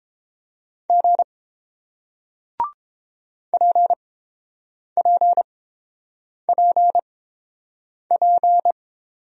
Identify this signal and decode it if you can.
Morse code, heard as a keyed tone, about 30 wpm, spaced as for 7 wpm, 700 Hz. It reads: G PPPP